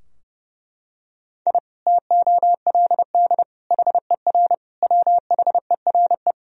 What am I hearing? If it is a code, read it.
Morse code, heard as a keyed tone, about 30 wpm, 700 Hz. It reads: I TOLD HER WHERE
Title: Sentences from Top 1000 Words - Review Part 2 - 30wpm